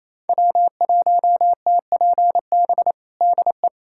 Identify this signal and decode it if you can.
Morse code, heard as a keyed tone, about 27 wpm, 700 Hz. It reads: W1TPB DE
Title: USA Callsign DE Callsign - 5 Characters - 28wpm